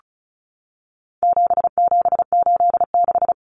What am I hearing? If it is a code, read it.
Morse code, heard as a keyed tone, about 34 wpm, 700 Hz. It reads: 7786
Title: Numbers - 4-Digit 35wpm